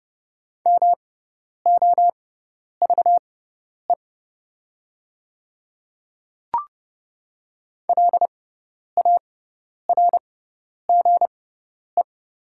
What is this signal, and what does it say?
Morse code, heard as a keyed tone, about 32 wpm, spaced as for 10 wpm, 700 Hz. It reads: MOVE LARGE
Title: 6x Character Spacing - Top 300 Words - 30wpm